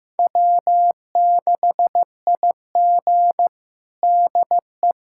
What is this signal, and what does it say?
Morse code, heard as a keyed tone, about 16 wpm, 700 Hz. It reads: W6IG DE
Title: USA Callsign DE Callsign - 4 Characters - 15wpm